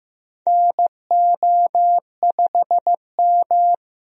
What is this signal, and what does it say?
Morse code, heard as a keyed tone, about 15 wpm, 700 Hz. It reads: NO5M
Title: USA Call Signs - 4 Characters - Repeated 3x - 15wpm